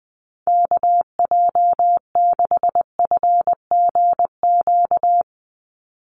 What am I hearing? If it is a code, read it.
Morse code, heard as a keyed tone, about 19 wpm, 700 Hz. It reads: KJ6FGQ
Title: USA Call Signs 20wpm